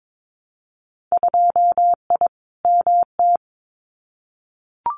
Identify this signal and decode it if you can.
Morse code, heard as a keyed tone, about 22 wpm, 700 Hz. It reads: 2I MT
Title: Winter Field Day Exchanges Encore 22wpm